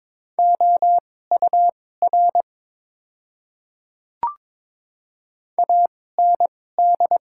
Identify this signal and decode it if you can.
Morse code, heard as a keyed tone, about 21 wpm, spaced as for 16 wpm, 700 Hz. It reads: OUR AND
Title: 2x Character Spacing - Top 300 Words - 22wpm